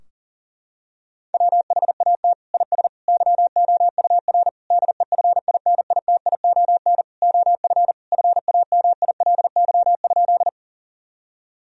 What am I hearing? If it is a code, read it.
Morse code, heard as a keyed tone, about 37 wpm, 700 Hz. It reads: WHAT IS YOUR DEFINITION OF FAMILY?
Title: Sentences from Top 1000 Words - Review Part 4 - 40wpm